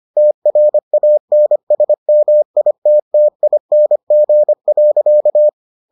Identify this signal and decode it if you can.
Morse code, heard as a keyed tone, about 25 wpm, 600 Hz. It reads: TRANSMITTING.